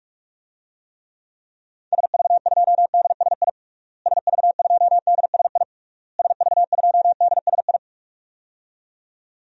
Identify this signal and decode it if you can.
Morse code, heard as a keyed tone, about 46 wpm, 700 Hz. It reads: SV2DSI SV2DSI SV2DSI
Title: DXCC-100 Call Signs - 6 Characters - Repeated 3x - 45wpm